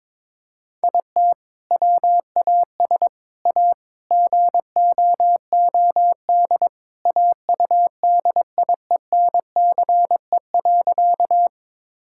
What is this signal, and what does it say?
Morse code, heard as a keyed tone, about 21 wpm, 700 Hz. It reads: IT WAS A GOOD AUDIENCE.
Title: Sentences from Top 1000 Words - Review Part 5 - 22wpm